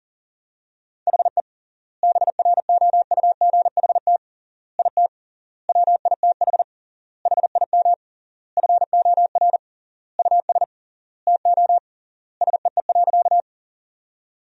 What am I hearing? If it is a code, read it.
Morse code, heard as a keyed tone, about 40 wpm, 700 Hz. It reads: HE BROUGHT IT WITH HIM FOR US TO SEE.
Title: 3x Word-Spacing - Sentences from Top 500 Words - 40wpm